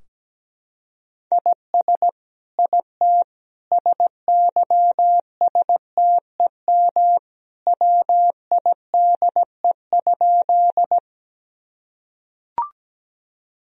Morse code, heard as a keyed tone, about 17 wpm, 700 Hz. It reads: IS IT SYSTEM WIDE?
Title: Sentences from Top 1000 Words - Review Part 3 - 17wpm